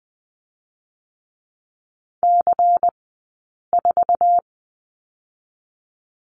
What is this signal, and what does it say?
Morse code, heard as a keyed tone, about 20 wpm, 700 Hz. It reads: C 4
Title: 120 Lesson 20d - Number 3 - Two Characters 20wpm